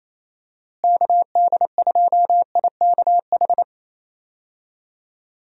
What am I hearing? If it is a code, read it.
Morse code, heard as a keyed tone, about 28 wpm, 700 Hz. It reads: KD2IKH
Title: USA Call Signs - 6 Characters - Repeated 2x - 28wpm